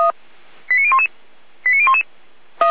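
Five-tone selective call (ZVEI), heard as quick runs of five tones, tones of about 70 ms. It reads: 89011 89011
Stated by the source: ZVEI five-tone selective call received by SDR.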